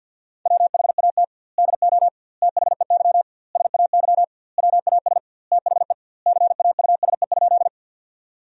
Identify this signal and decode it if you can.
Morse code, heard as a keyed tone, about 47 wpm, 700 Hz. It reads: WHAT DO THEY SAY WAS THE CAUSE?
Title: 2x Word-Spacing - Sentences from Top 200 Words - 50wpm